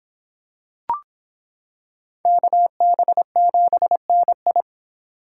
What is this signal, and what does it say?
Morse code, heard as a keyed tone, about 25 wpm, 700 Hz. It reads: KB7NI